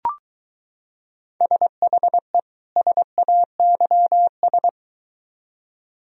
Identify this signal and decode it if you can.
Morse code, heard as a keyed tone, about 23 wpm, 700 Hz. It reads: SHE SAYS